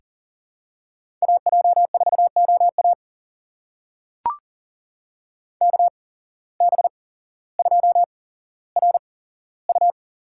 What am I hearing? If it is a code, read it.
Morse code, heard as a keyed tone, about 40 wpm, spaced as for 17 wpm, 700 Hz. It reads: AJ4OA KB2RU
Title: USA Call Signs - 8x Character Spacing - 40wpm